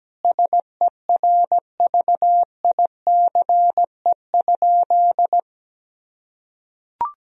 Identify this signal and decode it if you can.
Morse code, heard as a keyed tone, about 17 wpm, 700 Hz. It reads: SERVICE?
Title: Sentences from Top 1000 Words - Review Part 4 - 17wpm